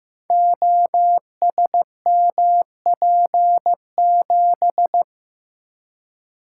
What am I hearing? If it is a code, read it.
Morse code, heard as a keyed tone, about 15 wpm, 700 Hz. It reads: OSMP7
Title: Code Groups - 5 Letter:Numbers - 15wpm